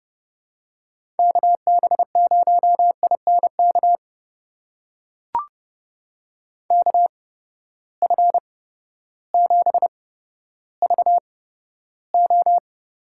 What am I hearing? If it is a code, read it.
Morse code, heard as a keyed tone, about 32 wpm, spaced as for 14 wpm, 700 Hz. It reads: KB0INK KF7VO
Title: USA Call Signs - 6 Characters - 8x Character Spacing - 30wpm